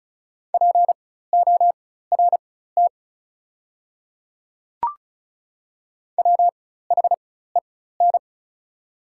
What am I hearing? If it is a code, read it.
Morse code, heard as a keyed tone, about 36 wpm, spaced as for 16 wpm, 690 Hz. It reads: PORT WHEN
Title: 4x Character Spacing - Top 200 Words - 35wpm